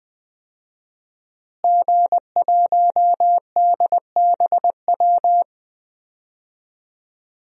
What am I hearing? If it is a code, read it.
Morse code, heard as a keyed tone, about 20 wpm, 700 Hz. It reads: G1DBW
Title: DXCC-100 Call Signs - 5 Characters - Repeated 3x - 20wpm